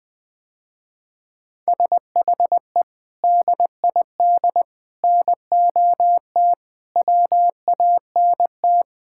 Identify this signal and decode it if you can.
Morse code, heard as a keyed tone, about 20 wpm, 700 Hz. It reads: SHE DID NOT WANT
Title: Sentences with 4-Word Phrases - Spoken Beginning - 20wpm